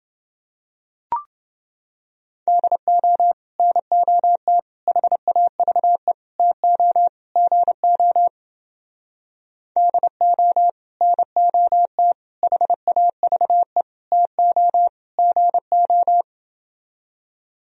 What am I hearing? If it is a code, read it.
Morse code, heard as a keyed tone, about 28 wpm, 700 Hz. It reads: DO NOT HAVE TO GO DO NOT HAVE TO GO